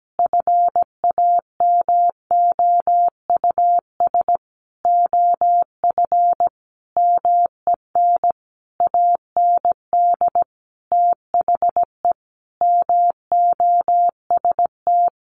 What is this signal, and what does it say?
Morse code, heard as a keyed tone, about 17 wpm, 700 Hz. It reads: FAMOUS OF MEN AND THE MOST
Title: Sentences from Top 1500 Words 17wpm